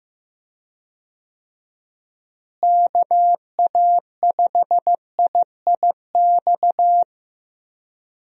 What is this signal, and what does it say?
Morse code, heard as a keyed tone, about 15 wpm, 700 Hz. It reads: KA5IIX
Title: USA Call Signs - 6 Characters - Repeated 3x - 15wpm